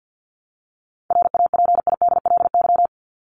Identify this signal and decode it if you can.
Morse code, heard as a keyed tone, about 50 wpm, 700 Hz. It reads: RAPIDLY
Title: LY Suffix - Top 100 Words with Base Words 50wpm